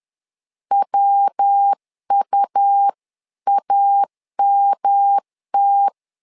Morse code, heard as a keyed tone, about 10 wpm, 800 Hz. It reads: WUAMT
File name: LEZIONI / 3 MWRO / 3 MW 10wpm 1959